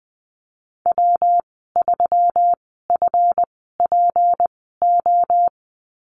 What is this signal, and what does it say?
Morse code, heard as a keyed tone, about 20 wpm, spaced as for 15 wpm, 700 Hz. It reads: W3FPO
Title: 186 Lesson 30g - Letter b - Call Signs 20wpm